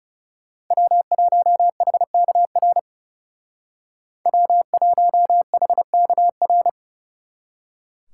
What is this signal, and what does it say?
Morse code, heard as a keyed tone, about 36 wpm, 700 Hz. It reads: W1HKR W1HKR
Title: Speed Racing - USA 1x3 Call Signs 30wpm